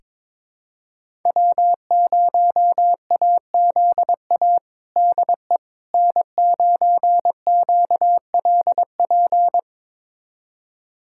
Morse code, heard as a keyed tone, about 21 wpm, 700 Hz. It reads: W0AZA DE N9QLP
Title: USA Callsign DE Callsign - 5 Characters - 22wpm